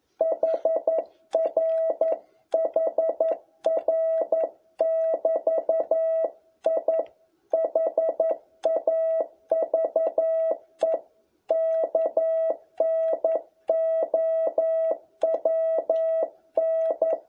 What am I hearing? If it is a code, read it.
Morse code, heard as a keyed tone, about 11 wpm, 660 Hz. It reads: HR HR<BT>I HAVE KNOWN
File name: jh8cbh / 2021cw